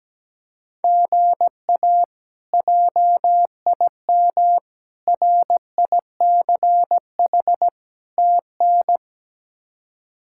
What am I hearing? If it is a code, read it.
Morse code, heard as a keyed tone, about 17 wpm, 700 Hz. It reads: GA JIM RICH TN
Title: K1USN SST QSO Practice - 17wpm